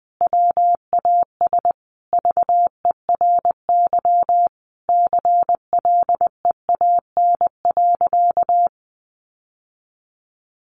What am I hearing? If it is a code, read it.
Morse code, heard as a keyed tone, about 20 wpm, 700 Hz. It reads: WAS VERY CLEAN.
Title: Sentences from Top 1100 Words 20wpm